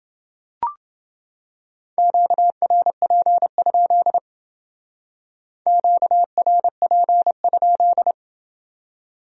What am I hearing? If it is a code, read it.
Morse code, heard as a keyed tone, about 28 wpm, 700 Hz. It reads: QRP? QRP?